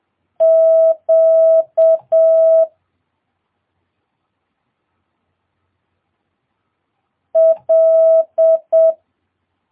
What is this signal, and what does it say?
Morse code, heard as a keyed tone, about 7 wpm, 650 Hz. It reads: Q L
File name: jh8cbh / 2022cw